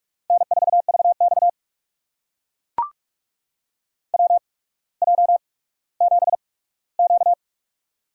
Morse code, heard as a keyed tone, about 42 wpm, spaced as for 19 wpm, 700 Hz. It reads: N4VX WJ7Q
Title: USA Call Signs - 4 Characters - 8x Character Spacing - 45wpm